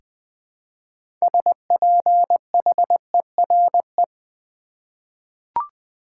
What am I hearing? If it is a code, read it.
Morse code, heard as a keyed tone, about 20 wpm, 700 Hz. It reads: SPHERE
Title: 6 Letter Words - First 3 Letters Given - 20wpm